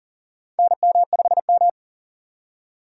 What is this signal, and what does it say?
Morse code, heard as a keyed tone, about 40 wpm, 700 Hz. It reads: NM5M
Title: USA Call Signs - 4 Characters - 8x Character Spacing - 40wpm